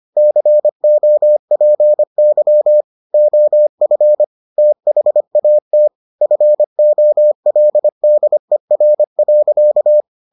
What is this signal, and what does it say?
Morse code, heard as a keyed tone, about 25 wpm, 600 Hz. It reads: COPY OF THAT FOLDER.